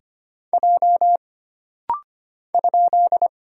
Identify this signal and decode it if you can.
Morse code, heard as a keyed tone, about 25 wpm, 700 Hz. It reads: J ?
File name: Single Character (Letter:Number:Punctuation:Prosign) - Rapid-Fire - 25wpm